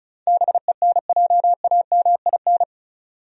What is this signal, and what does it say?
Morse code, heard as a keyed tone, about 36 wpm, 700 Hz. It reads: BENJAMIN
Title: USA Common Names - 35wpm